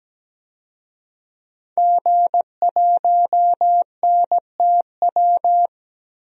Morse code, heard as a keyed tone, about 17 wpm, 700 Hz. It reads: G1NTW
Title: DXCC-100 Call Signs - 5 Characters - 17wpm